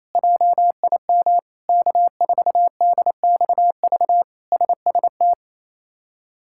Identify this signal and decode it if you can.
Morse code, heard as a keyed tone, about 28 wpm, 700 Hz. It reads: JIM K4DXV SST
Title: K1USN SST QSO Practice - 28wpm